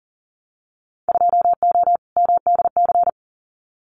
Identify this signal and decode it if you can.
Morse code, heard as a keyed tone, about 40 wpm, 700 Hz. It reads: 2O MDC
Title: Winter Field Day Exchanges 40wpm